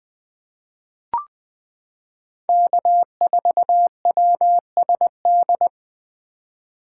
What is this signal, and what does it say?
Morse code, heard as a keyed tone, about 20 wpm, 700 Hz. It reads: K4WSD
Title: 2023 POTA Activator Call Signs - 20wpm